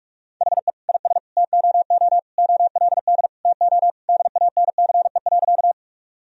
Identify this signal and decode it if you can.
Morse code, heard as a keyed tone, about 46 wpm, 700 Hz. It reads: HE IS TOO OLD TO DANCE.
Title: Sentences from Top 1000 Words - Review Part 4 - 45wpm